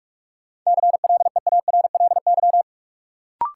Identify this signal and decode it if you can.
Morse code, heard as a keyed tone, about 46 wpm, 700 Hz. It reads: CLEARLY
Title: Sentences from Top 1300 Words - New Word Drill - 45wpm